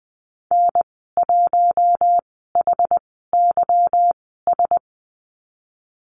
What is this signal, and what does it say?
Morse code, heard as a keyed tone, about 20 wpm, spaced as for 15 wpm, 700 Hz. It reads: N1HYS
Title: 161 Lesson 26g - Letter g - Call Signs 20wpm